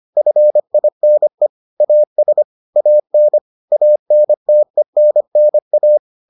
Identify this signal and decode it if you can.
Morse code, heard as a keyed tone, about 25 wpm, 600 Hz. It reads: FINE AS AN ANTENNA